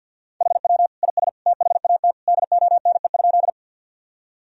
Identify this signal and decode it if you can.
Morse code, heard as a keyed tone, about 50 wpm, 700 Hz. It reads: HW IS THAT DONE?
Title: Sentences from Top 500 Words and Abbreviations 50wpm